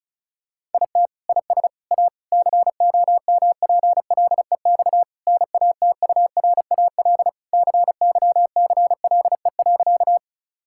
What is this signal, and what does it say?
Morse code, heard as a keyed tone, about 36 wpm, 700 Hz. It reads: IT IS A COMPLEX NATURAL CYCLE.
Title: Sentences from Top 1000 Words - Review Part 6 - 35wpm